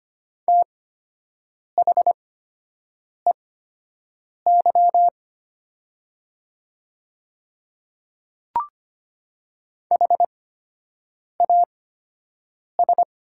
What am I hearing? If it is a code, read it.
Morse code, heard as a keyed tone, about 25 wpm, spaced as for 7 wpm, 700 Hz. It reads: THEY HAS
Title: 8x Character Spacing - Top 100 Words - 25wpm